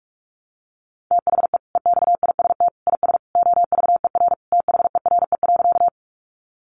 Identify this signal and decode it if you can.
Morse code, heard as a keyed tone, about 44 wpm, 700 Hz. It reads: THE EXIST IS OVER THERE.
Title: Sentences from Top 1100 Words 45wpm